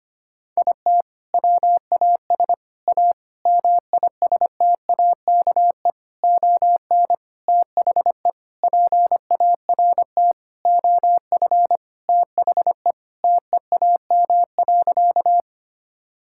Morse code, heard as a keyed tone, about 25 wpm, 700 Hz. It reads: IT WAS A MISTAKE ON THE PART OF THE TEAM.
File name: practice / 25wpm / Sentences with 4-Word Phrases - 25wpm